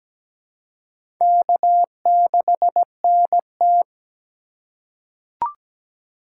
Morse code, heard as a keyed tone, about 17 wpm, 700 Hz. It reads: K6NT
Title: USA Call Signs - 3x Character Spacing - 17wpm